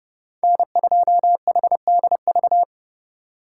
Morse code, heard as a keyed tone, about 31 wpm, 700 Hz. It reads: N2HDV
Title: USA Call Signs - 5 Characters - 3x Character Spacing - 30wpm